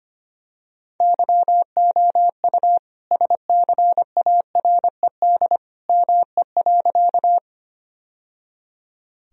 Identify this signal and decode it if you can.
Morse code, heard as a keyed tone, about 25 wpm, 700 Hz. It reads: YOU SCARED ME.
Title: Sentences from Top 1000 Words - Review Part 4 - 25wpm